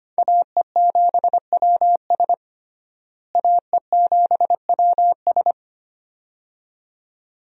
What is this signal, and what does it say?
Morse code, heard as a keyed tone, about 25 wpm, 700 Hz. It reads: AE7WS AE7WS
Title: USA Call Signs - 5 Characters - Repeated 2x - 25wpm